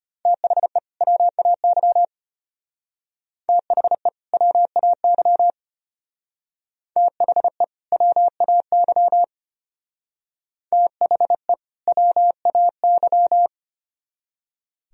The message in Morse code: THE WAY THE WAY THE WAY THE WAY

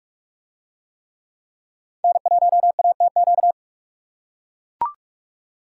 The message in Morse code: N1ATQ

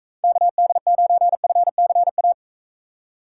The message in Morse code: KD9FCA

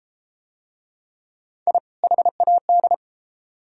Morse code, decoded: I HAD

